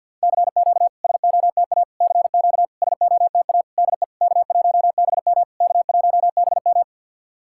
Morse code, heard as a keyed tone, about 50 wpm, 700 Hz. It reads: CQ SOTA CQ SOTA DE K1BM K1BM
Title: SOTA QSO Practice 50wpm